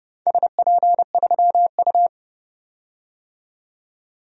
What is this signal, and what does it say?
Morse code, heard as a keyed tone, about 30 wpm, 700 Hz. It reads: SP3U
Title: DXCC-100 Call Signs - 4 Characters - 30wpm